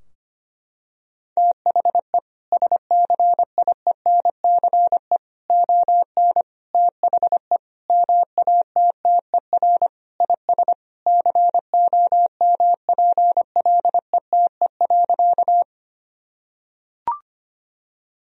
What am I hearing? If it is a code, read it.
Morse code, heard as a keyed tone, about 25 wpm, 700 Hz. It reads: THE SCIENCE ON THE MATTER IS COMPLETE.
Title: Sentences from Top 1000 Words - Review Part 2 - 25wpm